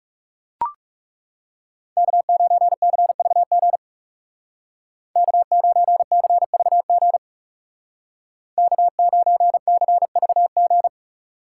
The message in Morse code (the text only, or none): K9CVG K9CVG K9CVG